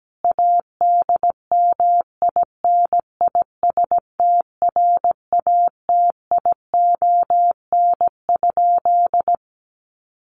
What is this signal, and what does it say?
Morse code, heard as a keyed tone, about 17 wpm, 700 Hz. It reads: ADMINISTRATION?
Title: Sentences from Top 1300 Words 17wpm